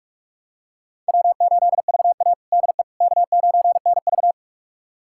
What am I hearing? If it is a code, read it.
Morse code, heard as a keyed tone, about 46 wpm, 700 Hz. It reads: W8VA DE K9NV
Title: USA Callsign DE Callsign - 4 Characters - 45wpm